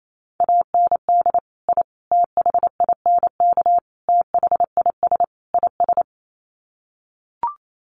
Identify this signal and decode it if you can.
Morse code, heard as a keyed tone, about 28 wpm, 700 Hz. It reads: AND I THINK THIS IS
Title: Sets of 5 Words Encore 28wpm